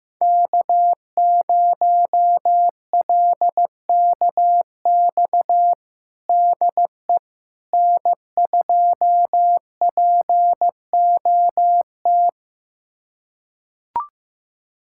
Morse code, heard as a keyed tone, about 15 wpm, 700 Hz. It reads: K0LKX DE N2POT
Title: USA Callsign DE Callsign - 5 Characters - 15wpm